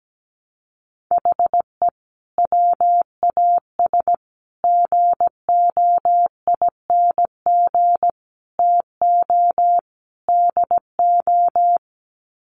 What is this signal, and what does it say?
Morse code, heard as a keyed tone, about 17 wpm, 700 Hz. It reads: HE WAS GOING TO DO
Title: Sets of 5 Words Encore2 17wpm